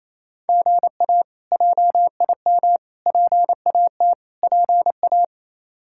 Morse code, heard as a keyed tone, about 28 wpm, 700 Hz. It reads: GA JIM PAT PA
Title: K1USN SST QSO Practice - 28wpm